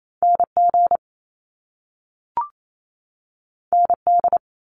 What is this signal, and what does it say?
Morse code, heard as a keyed tone, about 28 wpm, 700 Hz. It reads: NG ND
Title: Top Two Letter Combinations in Words 28wpm